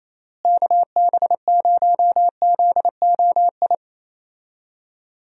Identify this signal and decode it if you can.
Morse code, heard as a keyed tone, about 27 wpm, 700 Hz. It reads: KB0ZOI
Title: USA Call Signs - 6 Characters - Repeated 2x - 28wpm